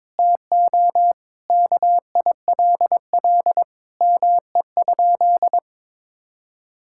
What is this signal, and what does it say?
Morse code, heard as a keyed tone, about 22 wpm, 700 Hz. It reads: TO KILL ME?
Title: Sentences from Top 1600 Words 22wpm